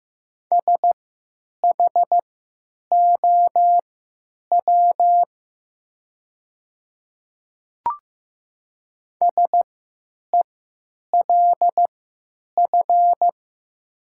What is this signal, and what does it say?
Morse code, heard as a keyed tone, about 15 wpm, spaced as for 8 wpm, 700 Hz. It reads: SHOW SELF
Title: 3x Character Spacing - Top 200 Words - 15wpm